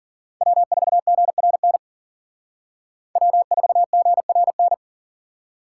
Morse code, heard as a keyed tone, about 51 wpm, 700 Hz. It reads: W4GRN W4GRN